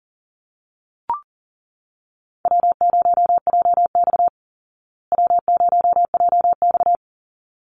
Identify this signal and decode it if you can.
Morse code, heard as a keyed tone, about 40 wpm, 700 Hz. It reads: W0JX W0JX
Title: Call Signs Repeated 3x - 40wpm